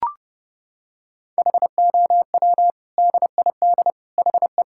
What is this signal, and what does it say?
Morse code, heard as a keyed tone, about 32 wpm, 700 Hz. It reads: HOW DID HE
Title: Sentences from Top 1600 Words 30wpm